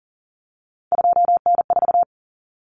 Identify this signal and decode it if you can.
Morse code, heard as a keyed tone, about 40 wpm, 700 Hz. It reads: 2N4
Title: Common Cut Numbers - 3-Digit 40wpm